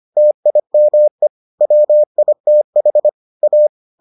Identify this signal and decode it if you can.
Morse code, heard as a keyed tone, about 25 wpm, 600 Hz. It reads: TIME WITH A